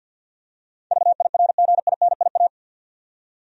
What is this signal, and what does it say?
Morse code, heard as a keyed tone, about 49 wpm, 700 Hz. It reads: VIRGINIA
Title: USA Common Names - 50wpm